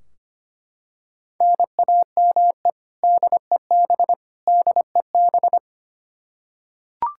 Morse code, heard as a keyed tone, about 25 wpm, 700 Hz. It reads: NAME DEB DEB
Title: Practice QSO - Exchange 1 - Beginning - 25wpm